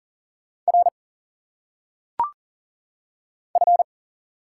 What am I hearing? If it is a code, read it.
Morse code, heard as a keyed tone, about 40 wpm, 700 Hz. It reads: R F
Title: Single Letters - Phonetic Pronunciation 40wpm